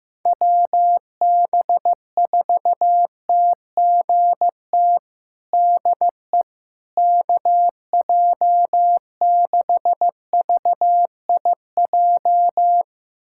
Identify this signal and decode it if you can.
Morse code, heard as a keyed tone, about 15 wpm, 700 Hz. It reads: WB4TGT DE KJ6VIJ
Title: USA Callsign DE Callsign - 6 Characters - 15wpm